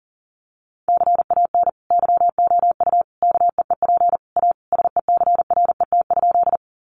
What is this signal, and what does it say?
Morse code, heard as a keyed tone, about 40 wpm, 700 Hz. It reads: CAN YOU KEEP A SECRET?
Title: Sentences from Top 1200 Words 40wpm